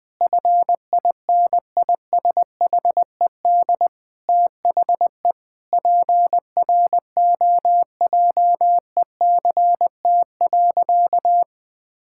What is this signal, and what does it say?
Morse code, heard as a keyed tone, about 21 wpm, 700 Hz. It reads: FINISHED THE PROJECT.
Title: Sentences with 5-Word Phrases - Spoken Beginning - 20wpm